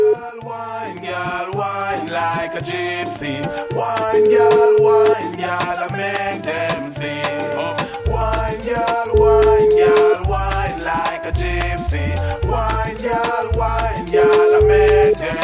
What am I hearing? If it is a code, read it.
Ringback tone (Europe, 425 Hz), with noise.